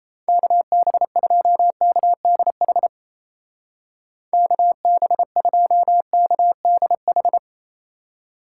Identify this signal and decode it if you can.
Morse code, heard as a keyed tone, about 36 wpm, 700 Hz. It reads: KB2KDH KB2KDH